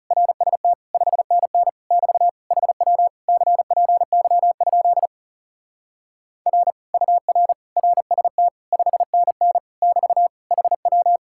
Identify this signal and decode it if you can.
Morse code, heard as a keyed tone, about 40 wpm, 700 Hz. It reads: RST 5NN <BT> HW CPY? R UR RST 5NN <BT> HW